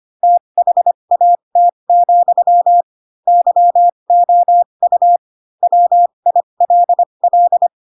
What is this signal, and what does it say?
Morse code, heard as a keyed tone, about 25 wpm, spaced as for 22 wpm, 700 Hz. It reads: THAT, YOU WILL
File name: Pinocchio / Pinocchio ch4 18 25 ik1hge